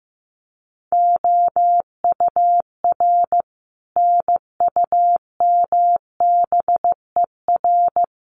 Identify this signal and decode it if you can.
Morse code, heard as a keyed tone, about 15 wpm, 700 Hz. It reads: OUR NUMBER
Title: Sets of 5 Words Encore2 15wpm